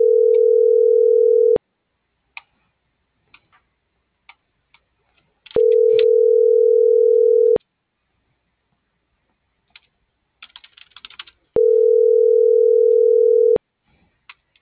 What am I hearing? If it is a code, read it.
Ringback tone (North America), 440 + 480 Hz.